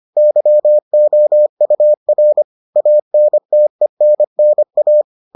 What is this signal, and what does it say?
Morse code, heard as a keyed tone, about 25 wpm, 600 Hz. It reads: YOUR ANTENNA